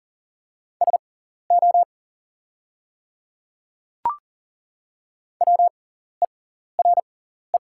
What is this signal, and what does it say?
Morse code, heard as a keyed tone, about 40 wpm, spaced as for 14 wpm, 690 Hz. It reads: SO WERE
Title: 6x Character Spacing - Top 100 Words - 40wpm